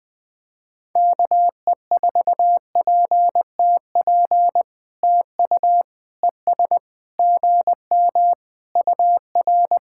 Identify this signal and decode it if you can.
Morse code, heard as a keyed tone, about 21 wpm, 700 Hz. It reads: KE4PTP TU ES GM UR
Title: POTA QSO Practice - 20wpm